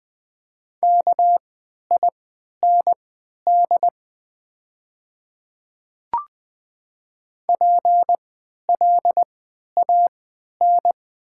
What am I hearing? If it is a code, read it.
Morse code, heard as a keyed tone, about 20 wpm, spaced as for 11 wpm, 700 Hz. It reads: KIND PLAN
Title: 3x Character Spacing - Top 500 Words - 20wpm